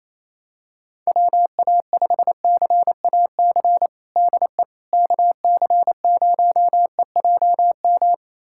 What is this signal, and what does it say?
Morse code, heard as a keyed tone, about 28 wpm, 700 Hz. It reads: WA5CAC DE KC0EJM
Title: USA Callsign DE Callsign 28wpm